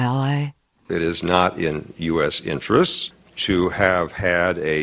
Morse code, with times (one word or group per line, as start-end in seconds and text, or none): none